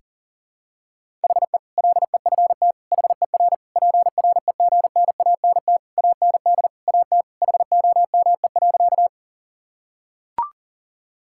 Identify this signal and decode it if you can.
Morse code, heard as a keyed tone, about 40 wpm, 700 Hz. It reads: HE LEFT HER PREGNANT AND AT HOME.
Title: Sentences from Top 1000 Words - Review Part 6 - 40wpm